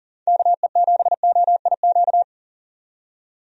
Morse code, heard as a keyed tone, about 40 wpm, 700 Hz. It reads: KE7OIQ
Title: USA Call Signs - 6 Characters - 40wpm